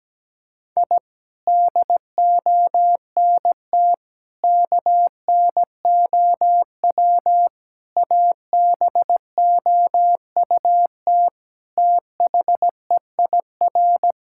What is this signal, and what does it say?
Morse code, heard as a keyed tone, about 17 wpm, 700 Hz. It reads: I DONT KNOW ABOUT THEIR
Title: Sentences from Top 1000 Words - Review Part 5 - 17wpm